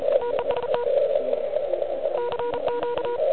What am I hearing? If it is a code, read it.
Morse code, heard as a keyed tone, about 33 wpm, 960 Hz, with noise. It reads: TEST CQ
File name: ARCHIV / QCX / QCX Pileup